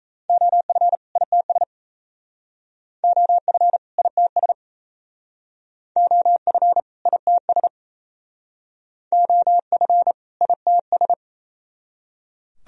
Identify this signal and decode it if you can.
Morse code, heard as a keyed tone, about 34 wpm, 700 Hz. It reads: OF ITS OF ITS OF ITS OF ITS
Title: Speed-Racing - Sets of 2 Words 28wpm